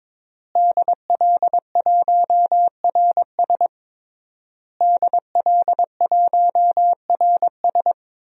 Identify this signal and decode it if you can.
Morse code, heard as a keyed tone, about 21 wpm, 700 Hz. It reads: DL1RS DL1RS
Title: DXCC-100 Call Signs - 5 Characters - Repeated 3x - 22wpm